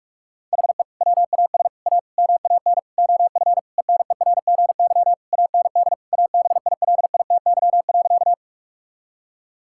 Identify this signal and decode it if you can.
Morse code, heard as a keyed tone, about 46 wpm, 700 Hz. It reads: HE WAS A MAN OF ENERGY AND ABILITY.
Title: Sentences from Top 1000 Words - Review Part 6 - 45wpm